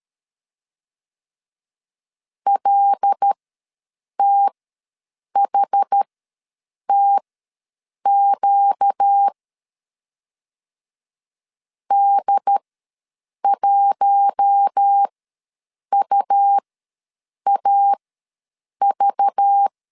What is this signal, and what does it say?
Morse code, heard as a keyed tone, about 13 wpm, spaced as for 7 wpm, 800 Hz. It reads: LTHTQ D1UAV
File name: LEZIONI / 5 LBVC / 5 LBV ws 12wpm 3015